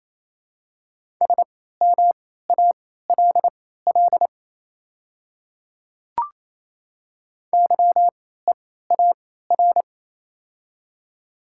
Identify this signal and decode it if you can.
Morse code, heard as a keyed tone, about 28 wpm, spaced as for 16 wpm, 700 Hz. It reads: SMALL YEAR
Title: 3x Character Spacing - Top 300 Words - 28wpm